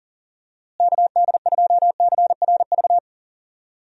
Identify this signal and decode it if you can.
Morse code, heard as a keyed tone, about 40 wpm, 700 Hz. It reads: KD2CRV
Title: USA Call Signs - 6 Characters - Repeated 2x - 40wpm